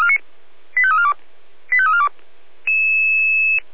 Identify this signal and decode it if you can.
Five-tone selective call (ZVEI), heard as quick runs of five tones, tones of about 70 ms. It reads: 85342 85342